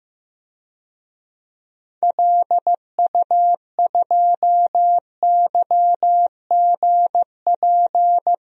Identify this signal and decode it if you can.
Morse code, heard as a keyed tone, about 15 wpm, 700 Hz. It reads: LU2YGP
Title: DXCC-100 Call Signs - 6 Characters - 15wpm